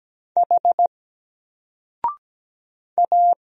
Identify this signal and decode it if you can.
Morse code, heard as a keyed tone, about 17 wpm, 700 Hz. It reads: H A
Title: Single Letter-Number Alt 17wpm